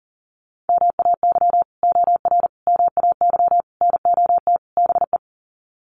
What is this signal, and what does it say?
Morse code, heard as a keyed tone, about 40 wpm, 700 Hz. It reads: MAY OR MAY NOT BE
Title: Sets of 5 Words 40wpm